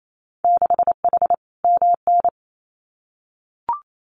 Morse code, heard as a keyed tone, about 28 wpm, 700 Hz. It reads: 6H MN
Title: Winter Field Day Exchanges Encore 28wpm